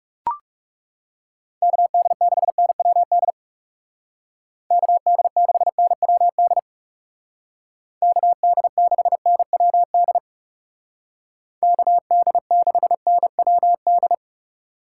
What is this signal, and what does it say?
Morse code, heard as a keyed tone, about 37 wpm, 700 Hz. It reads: KD6NWD KD6NWD KD6NWD KD6NWD
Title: Speed Racing - USA 2x3 Call Signs 30wpm